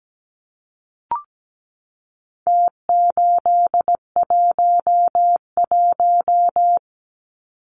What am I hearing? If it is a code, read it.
Morse code, heard as a keyed tone, about 17 wpm, 700 Hz. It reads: T811
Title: Numbers - Contest Style - 4-Digit 17wpm